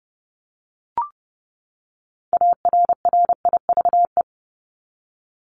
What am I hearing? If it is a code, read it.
Morse code, heard as a keyed tone, about 30 wpm, 700 Hz. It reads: ARRIVE
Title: ED Suffix - Top 100 Words with Base Words 30wpm